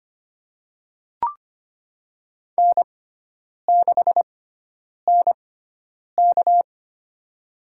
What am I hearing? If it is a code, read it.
Morse code, heard as a keyed tone, about 25 wpm, spaced as for 9 wpm, 700 Hz. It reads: N6NK
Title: USA Call Signs - 4 Characters - 6x Character Spacing - 25wpm